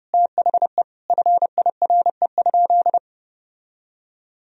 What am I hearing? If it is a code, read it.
Morse code, heard as a keyed tone, about 30 wpm, 700 Hz. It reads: THE FIRE?
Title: Sentences from Top 1000 Words - Review Part 3 - 30wpm